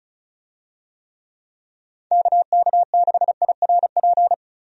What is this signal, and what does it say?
Morse code, heard as a keyed tone, about 34 wpm, 700 Hz. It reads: KK6IRP
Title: USA Call Signs - 6 Characters - 35wpm